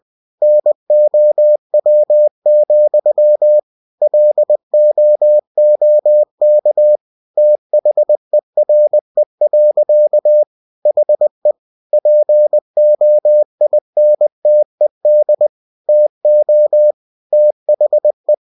Morse code, heard as a keyed tone, about 20 wpm, 600 Hz. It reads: NOW, LOOK THERE. HE POINTED TO THE
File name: winnie-20wpm